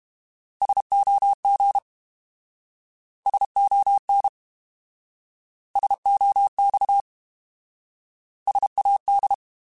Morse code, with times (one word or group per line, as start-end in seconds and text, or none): 0.6-1.8 SOG
3.3-4.3 SON
5.8-7.0 SOX
8.5-9.3 SAD